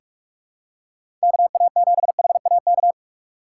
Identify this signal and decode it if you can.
Morse code, heard as a keyed tone, about 46 wpm, 700 Hz. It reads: KA7HAK
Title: USA Call Signs - 6 Characters - 3x Character Spacing - 45wpm